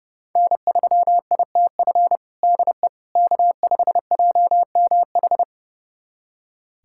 Morse code, heard as a keyed tone, about 30 wpm, 700 Hz. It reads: N3ITF DE K5JMH
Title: USA Callsign DE Callsign - 5 Characters - 30wpm